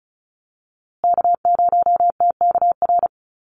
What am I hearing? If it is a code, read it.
Morse code, heard as a keyed tone, about 34 wpm, 700 Hz. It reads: K0TKR
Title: USA Call Signs Repeated 2x 35wpm